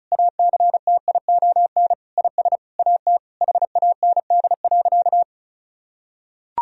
Morse code, heard as a keyed tone, about 36 wpm, 700 Hz. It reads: ACTION IS AT HAND.
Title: Sentences from Top 1000 Words - Review Part 4 - 35wpm